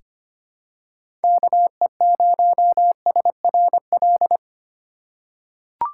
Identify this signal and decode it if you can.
Morse code, heard as a keyed tone, about 25 wpm, 700 Hz. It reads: KE0SRL
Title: K1USN SST QSO Practice - 25wpm